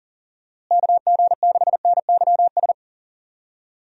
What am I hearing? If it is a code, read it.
Morse code, heard as a keyed tone, about 39 wpm, 700 Hz. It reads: KG6NYS
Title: USA Call Signs - 6 Characters - Repeated 3x - 40wpm